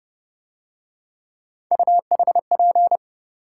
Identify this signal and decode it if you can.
Morse code, heard as a keyed tone, about 29 wpm, 700 Hz. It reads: UHP